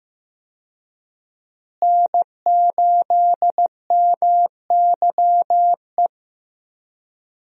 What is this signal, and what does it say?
Morse code, heard as a keyed tone, about 16 wpm, 700 Hz. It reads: N8MYE
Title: USA Call Signs - 5 Characters - Repeated 2x - 15wpm